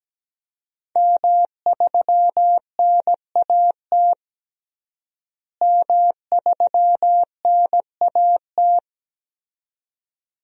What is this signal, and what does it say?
Morse code, heard as a keyed tone, about 17 wpm, 700 Hz. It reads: M3NAT M3NAT
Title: DXCC-100 Call Signs - 5 Characters - Repeated 2x - 17wpm